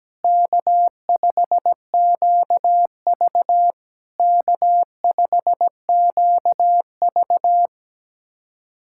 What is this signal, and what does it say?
Morse code, heard as a keyed tone, about 17 wpm, 700 Hz. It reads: K5QV K5QV
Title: SOTA QSO Practice 17wpm